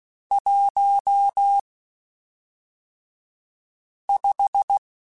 Morse code, heard as a keyed tone, about 16 wpm, spaced as for 3 wpm, 800 Hz. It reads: 15
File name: (spaced) / QSOs Spaced-mp3s / WD0BGP 1 @5 sp